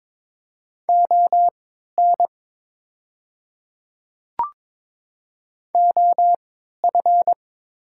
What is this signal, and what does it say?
Morse code, heard as a keyed tone, about 22 wpm, spaced as for 12 wpm, 700 Hz. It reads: ON OF
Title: 3x Character Spacing - Top 200 Words - 22wpm